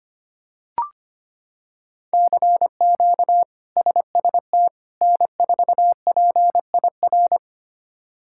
Morse code, heard as a keyed tone, about 25 wpm, 700 Hz. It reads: CQ SST N4PIR
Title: K1USN SST QSO Practice - 25wpm